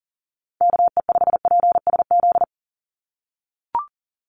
Morse code, heard as a keyed tone, about 40 wpm, 700 Hz. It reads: KE5PSZ
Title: USA Call Signs 40wpm